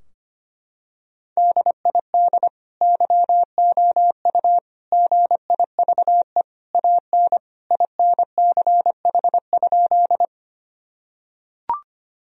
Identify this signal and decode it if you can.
Morse code, heard as a keyed tone, about 25 wpm, 700 Hz. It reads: DID YOU GIVE AN INCH?
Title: Sentences from Top 1000 Words - Review Part 3 - 25wpm